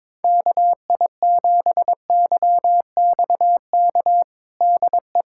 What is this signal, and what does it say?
Morse code, heard as a keyed tone, about 22 wpm, 700 Hz. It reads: KI7YXK DE
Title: USA Callsign DE Callsign 22wpm